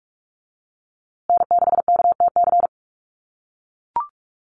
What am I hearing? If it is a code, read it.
Morse code, heard as a keyed tone, about 44 wpm, 700 Hz. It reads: N6KTC
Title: USA Call Signs Repeated 3x 45wpm